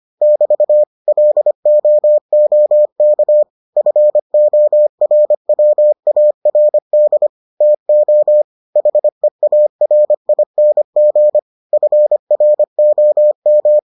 Morse code, heard as a keyed tone, about 25 wpm, 600 Hz. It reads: <BT> LOOK FORWARD TO HEARING FROM